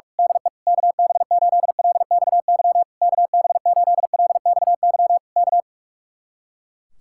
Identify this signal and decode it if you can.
Morse code, heard as a keyed tone, about 46 wpm, 700 Hz. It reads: DE KB8LXY KB8LXY K
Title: Speed-Racing Common QSO Phrases Encore - 45wpm